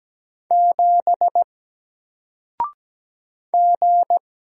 Morse code, heard as a keyed tone, about 17 wpm, 700 Hz. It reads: 7 G
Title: Single Letter-Number Alt 17wpm